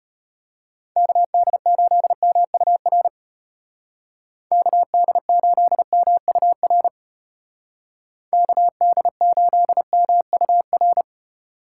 Morse code, heard as a keyed tone, about 29 wpm, 700 Hz. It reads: KD8MUR KD8MUR KD8MUR
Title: Speed Racing - USA Call Signs 25wpm